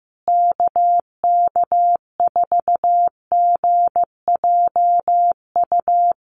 Morse code, heard as a keyed tone, about 15 wpm, 700 Hz. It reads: KK4GJU
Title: USA Call Signs Repeated 3x 15wpm